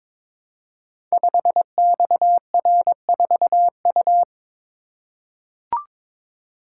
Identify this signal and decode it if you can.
Morse code, heard as a keyed tone, about 22 wpm, 700 Hz. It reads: 5XR4U
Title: Code Groups - 5 Letter:Numbers - 22wpm